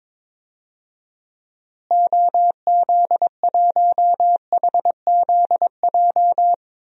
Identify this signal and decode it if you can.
Morse code, heard as a keyed tone, about 22 wpm, 700 Hz. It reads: OZ1HZJ
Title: DXCC-100 Call Signs - 6 Characters - Repeated 3x - 22wpm